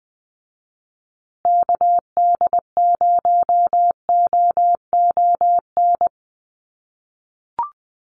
Morse code, heard as a keyed tone, about 20 wpm, 700 Hz. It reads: KD0OON
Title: USA Call Signs Repeated 2x 20wpm